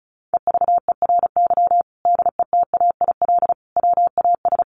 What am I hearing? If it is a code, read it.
Morse code, heard as a keyed tone, about 34 wpm, 700 Hz. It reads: EVERY DETAIL WAS
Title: Sentences from Top 1500 Words 35wpm